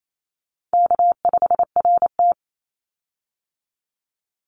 Morse code, heard as a keyed tone, about 28 wpm, 700 Hz. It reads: K5RT
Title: Easy Call Signs - 28wpm